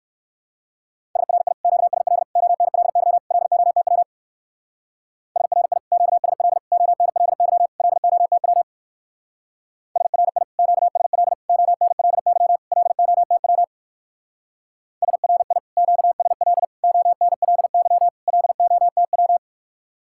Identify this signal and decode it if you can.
Morse code, heard as a keyed tone, about 55 wpm, 700 Hz. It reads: SRI QSL ONLY LOTW SRI QSL ONLY LOTW SRI QSL ONLY LOTW SRI QSL ONLY LOTW
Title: Speed-Racing Common QSO Phrases - 45wpm